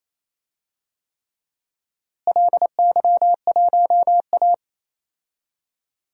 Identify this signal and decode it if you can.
Morse code, heard as a keyed tone, about 28 wpm, 700 Hz. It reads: LY1A
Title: DXCC-100 Call Signs - 4 Characters - Repeated 3x - 28wpm